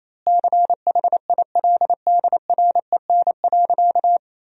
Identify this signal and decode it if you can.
Morse code, heard as a keyed tone, about 28 wpm, 700 Hz. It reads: CHILDREN.
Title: 2x Word-Spacing - Sentences from Top 400 Words - 28wpm